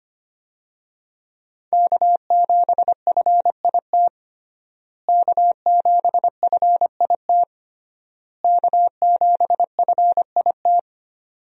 Morse code, heard as a keyed tone, about 25 wpm, 700 Hz. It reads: K7FIT K7FIT K7FIT
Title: USA Call Signs - 5 Characters - Repeated 3x - 25wpm